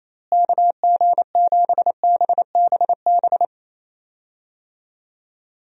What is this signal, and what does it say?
Morse code, heard as a keyed tone, about 28 wpm, 700 Hz. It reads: KG7BBB
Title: USA Call Signs - 6 Characters - Repeated 2x - 28wpm